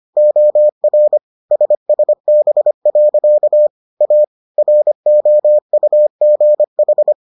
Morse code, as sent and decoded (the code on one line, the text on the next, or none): --- .-. / ... ... -... .-.-.- / .- / .-. --- ..- --. ....
OR SSB. A ROUGH